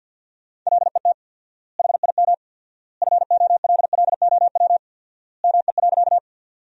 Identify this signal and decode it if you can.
Morse code, heard as a keyed tone, about 50 wpm, 700 Hz. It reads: LET HIM FOLLOW ME.
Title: 4x Word-Spacing - Sentences from Top 300 Words - 50wpm